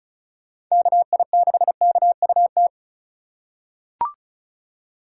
Morse code, heard as a keyed tone, about 34 wpm, 700 Hz. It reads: KI6KUT
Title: USA Call Signs - 6 Characters - 4x Character Spacing - 35wpm